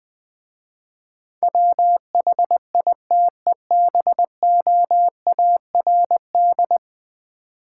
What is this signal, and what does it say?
Morse code, heard as a keyed tone, about 20 wpm, 700 Hz. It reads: WHITEBOARD
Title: Easy Words with Hints - 20wpm